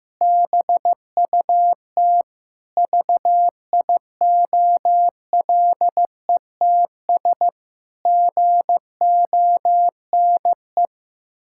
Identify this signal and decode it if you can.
Morse code, heard as a keyed tone, about 16 wpm, 700 Hz. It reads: BUT VIOLETS GONE
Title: Pangram Challenge v2 - 15wpm